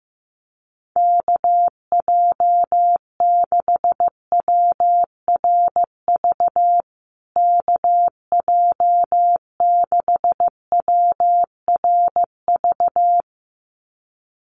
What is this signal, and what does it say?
Morse code, heard as a keyed tone, about 15 wpm, 700 Hz. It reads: KJ6WRV KJ6WRV
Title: USA Call Signs Repeated 2x 15wpm